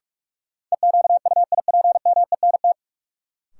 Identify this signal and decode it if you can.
Morse code, heard as a keyed tone, about 46 wpm, 700 Hz. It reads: EQUIPMENT
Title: Sentences from Top 700 Words - New Word Drill - 45wpm